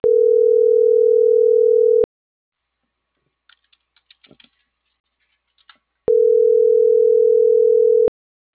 Ringback tone (North America), 440 + 480 Hz.